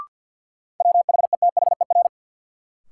Morse code, heard as a keyed tone, about 53 wpm, 700 Hz. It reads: WHETHER
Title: Sets of 3 Words - New Word Drill - 50wpm